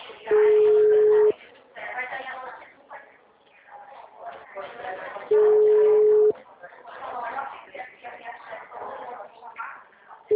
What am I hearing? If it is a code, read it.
Ringback tone (Europe, 425 Hz), with speech.